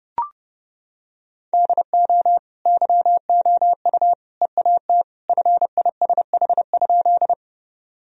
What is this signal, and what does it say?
Morse code, heard as a keyed tone, about 32 wpm, 700 Hz. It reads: DO YOU EAT FISH?
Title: Sentences from Top 1000 Words - Review Part 1 - 30wpm